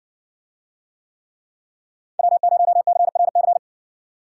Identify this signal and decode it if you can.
Morse code, heard as a keyed tone, about 60 wpm, 700 Hz. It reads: W0CRC